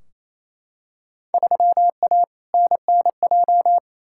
Morse code, heard as a keyed tone, about 28 wpm, 700 Hz. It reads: 3A NNJ
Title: Field Day Exchanges Encore - 28wpm